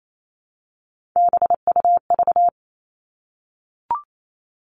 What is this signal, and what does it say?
Morse code, heard as a keyed tone, about 28 wpm, 700 Hz. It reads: BUV
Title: All Cut Numbers - 3-Digit 28wpm